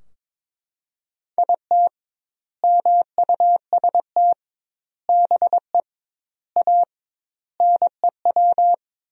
Morse code, heard as a keyed tone, about 22 wpm, 700 Hz. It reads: IT MUST BE A NEW
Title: 2x Word-Spacing - Sentences from Top 500 Words - 22wpm